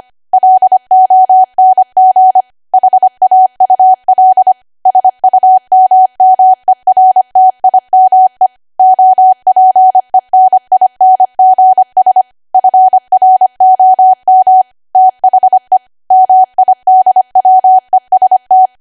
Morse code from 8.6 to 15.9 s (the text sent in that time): OPENINGS FROM THE